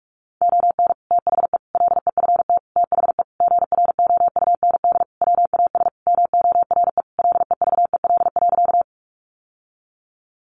Morse code, heard as a keyed tone, about 44 wpm, 700 Hz. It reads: ON THE LEFT THE GROUND WAS MORE LEVEL.
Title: Sentences from Top 700 Words 45wpm